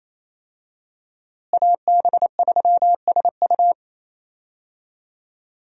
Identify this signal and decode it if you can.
Morse code, heard as a keyed tone, about 28 wpm, 700 Hz. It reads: AB3SU
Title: USA Call Signs - 5 Characters - Repeated 3x - 28wpm